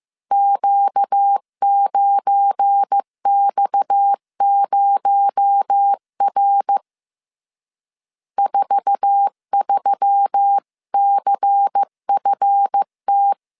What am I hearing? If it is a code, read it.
Morse code, heard as a keyed tone, about 15 wpm, 800 Hz. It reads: Q9X0R 43CFT
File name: CW / DET / 1 9 1500C S C 14wpm 3245